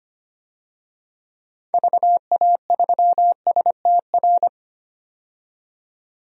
Morse code, heard as a keyed tone, about 25 wpm, 700 Hz. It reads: VA3STR
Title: DXCC-100 Call Signs - 6 Characters - 25wpm